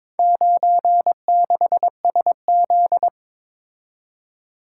Morse code, heard as a keyed tone, about 22 wpm, 700 Hz. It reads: 96SZ
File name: practice / 22wpm / Four Letter-Number Combinations - 22wpm